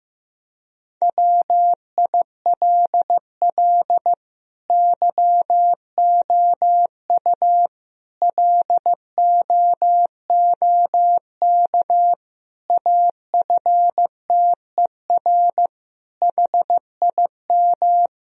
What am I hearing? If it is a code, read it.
Morse code, heard as a keyed tone, about 15 wpm, 700 Hz. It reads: WILL YOU LOOK AFTER HIM
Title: Sentences from Top 1000 Words - Review Part 1 - 15wpm